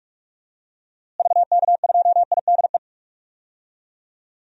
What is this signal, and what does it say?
Morse code, heard as a keyed tone, about 43 wpm, 700 Hz. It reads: VK2IDE
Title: DXCC-100 Call Signs - 6 Characters - Repeated 2x - 45wpm